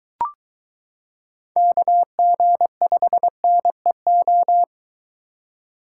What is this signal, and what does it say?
Morse code, heard as a keyed tone, about 23 wpm, 700 Hz. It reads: KG5NEO